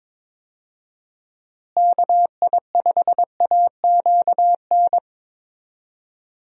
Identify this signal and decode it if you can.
Morse code, heard as a keyed tone, about 22 wpm, 700 Hz. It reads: KI5AQN
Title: USA Call Signs - 6 Characters - Repeated 2x - 22wpm